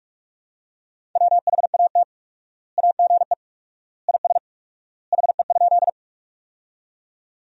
Morse code, heard as a keyed tone, about 46 wpm, 700 Hz. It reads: WHAT AGE IS HE?
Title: 4x Word-Spacing - Sentences from Top 500 Words - 45wpm